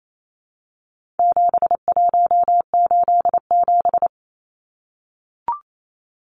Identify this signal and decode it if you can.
Morse code, heard as a keyed tone, about 28 wpm, 700 Hz. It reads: 7187
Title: Numbers - 4-Digit 28wpm